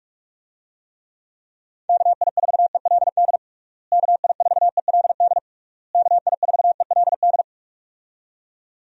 Morse code, heard as a keyed tone, about 46 wpm, 700 Hz. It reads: KI4ELD KI4ELD KI4ELD
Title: USA Call Signs - 6 Characters - Repeated 3x - 45wpm